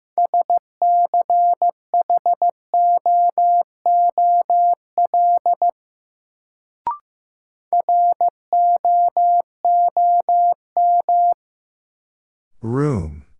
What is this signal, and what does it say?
Morse code, heard as a keyed tone, about 15 wpm, 700 Hz. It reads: SCHOOL ROOM
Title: Sentences from Top 300 Words - New Word Drill - 15wpm